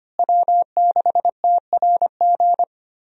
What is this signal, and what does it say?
Morse code, heard as a keyed tone, about 25 wpm, 700 Hz. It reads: W6TRG
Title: USA Call Signs - 5 Characters - Repeated 2x - 25wpm